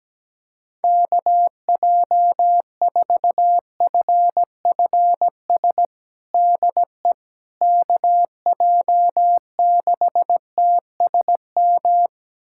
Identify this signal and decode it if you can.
Morse code, heard as a keyed tone, about 17 wpm, 700 Hz. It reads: KJ4FFS DE KJ6TSM
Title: USA Callsign DE Callsign - 6 Characters - 17wpm